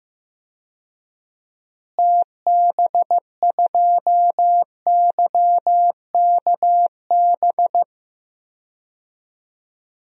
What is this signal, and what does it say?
Morse code, heard as a keyed tone, about 15 wpm, 700 Hz. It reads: TB2YKB
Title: DXCC-100 Call Signs - 6 Characters - Repeated 2x - 15wpm